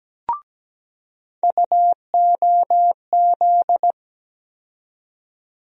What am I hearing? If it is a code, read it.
Morse code, heard as a keyed tone, about 17 wpm, 700 Hz. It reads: UOZ
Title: USA Call Sign Postfixes - 3 Letters 17wpm